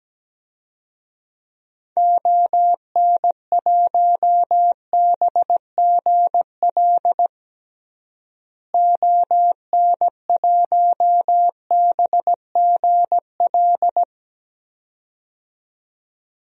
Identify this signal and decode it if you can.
Morse code, heard as a keyed tone, about 17 wpm, 700 Hz. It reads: ON1BGL ON1BGL
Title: DXCC-100 Call Signs - 6 Characters - Repeated 2x - 17wpm